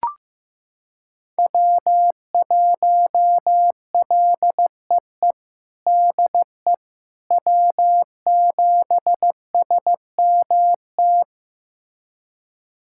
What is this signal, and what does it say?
Morse code, heard as a keyed tone, about 15 wpm, 700 Hz. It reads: W1LEE DE W7SMT
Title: USA Callsign DE Callsign - 5 Characters - 15wpm